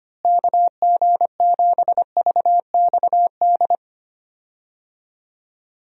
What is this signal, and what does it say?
Morse code, heard as a keyed tone, about 25 wpm, 700 Hz. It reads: KG7VXD
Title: USA Call Signs - 6 Characters - Repeated 3x - 25wpm